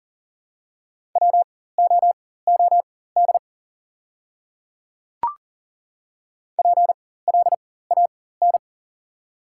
Morse code, heard as a keyed tone, about 40 wpm, spaced as for 19 wpm, 690 Hz. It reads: WOOD PLAN
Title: 4x Character Spacing - Top 500 Words - 40wpm